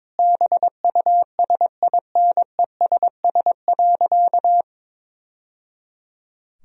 Morse code, heard as a keyed tone, about 22 wpm, 700 Hz. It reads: BUSINESS.
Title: Sentences from Top 1000 Words - Review Part 4 - 22wpm